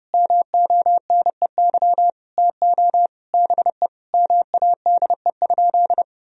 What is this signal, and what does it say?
Morse code, heard as a keyed tone, about 32 wpm, 700 Hz. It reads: MONEY TO BE MADE?
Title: Sentences from Top 1000 Words - Review Part 4 - 30wpm